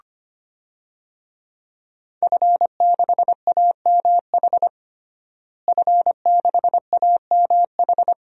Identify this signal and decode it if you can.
Morse code, heard as a keyed tone, about 25 wpm, 700 Hz. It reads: F6AMH F6AMH
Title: DXCC-100 Call Signs - 5 Characters - Repeated 2x - 25wpm